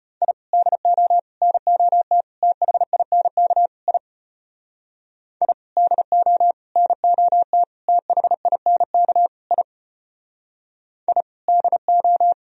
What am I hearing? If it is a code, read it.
Morse code, heard as a keyed tone, about 33 wpm, 700 Hz. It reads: I DO NOT THINK I I DO NOT THINK I I DO